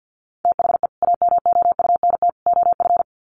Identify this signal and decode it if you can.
Morse code, heard as a keyed tone, about 49 wpm, 700 Hz. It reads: THE AMOUNT OF
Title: Sets of 3 Words 50wpm